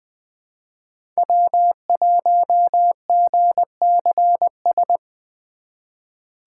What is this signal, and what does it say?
Morse code, heard as a keyed tone, about 20 wpm, 700 Hz. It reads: W1GCS
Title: K1USN SST QSO Practice - 20wpm